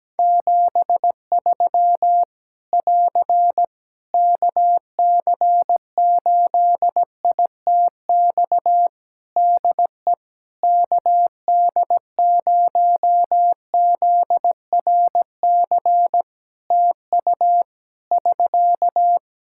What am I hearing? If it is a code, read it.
Morse code, heard as a keyed tone, about 17 wpm, 700 Hz. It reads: 73 <AR> KC8ITX DE KD0ZRC TU <SK>
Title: Practice QSO - Exchange 4 - Ending - 17wpm